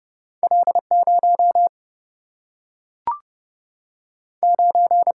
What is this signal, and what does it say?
Morse code, heard as a keyed tone, about 30 wpm, 700 Hz. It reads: L 0 9
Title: Single-Letter Single-Number 30wpm